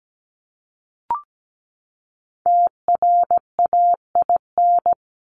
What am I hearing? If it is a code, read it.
Morse code, heard as a keyed tone, about 17 wpm, 700 Hz. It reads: TRAIN
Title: Top 1751-2000 Words 17wpm